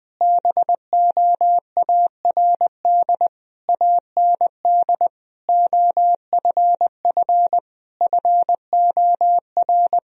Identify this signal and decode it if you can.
Morse code, heard as a keyed tone, about 20 wpm, 700 Hz. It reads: BOARD AND OFF FOR
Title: Sentences from Top 1000 Words - Review Part 4 - 20wpm